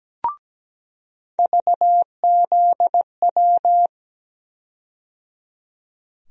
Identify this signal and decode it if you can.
Morse code, heard as a keyed tone, about 17 wpm, 700 Hz. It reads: VZW
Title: USA Call Sign Postfixes - 3 Letters 17wpm